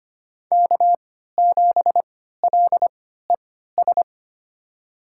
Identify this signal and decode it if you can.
Morse code, heard as a keyed tone, about 25 wpm, spaced as for 14 wpm, 700 Hz. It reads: K7LES
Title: USA Call Signs - 5 Characters - 3x Character Spacing - 25wpm